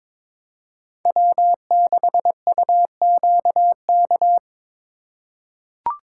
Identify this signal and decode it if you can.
Morse code, heard as a keyed tone, about 22 wpm, 700 Hz. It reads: W6UQK
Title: USA Call Signs - 4x Character Spacing - 22wpm